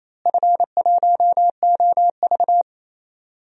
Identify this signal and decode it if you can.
Morse code, heard as a keyed tone, about 28 wpm, 700 Hz. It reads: F1OV
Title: DXCC-100 Call Signs - 4 Characters - Repeated 3x - 28wpm